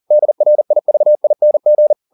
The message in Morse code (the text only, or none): DRIVING